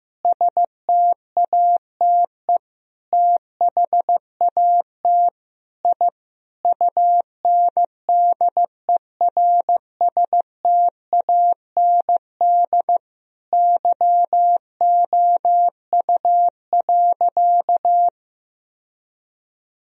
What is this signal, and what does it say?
Morse code, heard as a keyed tone, about 16 wpm, 700 Hz. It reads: STATE THAT I UNDERSTAND YOU.
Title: Sentences from Top 1000 Words - Review Part 2 - 15wpm